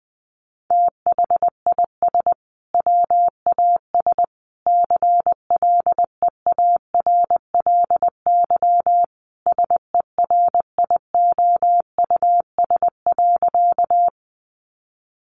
Morse code, heard as a keyed tone, about 20 wpm, 700 Hz. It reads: THIS WAS CLEARLY SERIOUS.
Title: Sentences from Top 1300 Words 20wpm